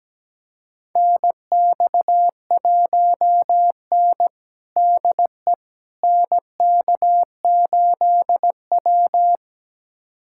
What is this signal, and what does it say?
Morse code, heard as a keyed tone, about 17 wpm, 700 Hz. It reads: NX1N DE NK8W
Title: USA Callsign DE Callsign - 4 Characters - 17wpm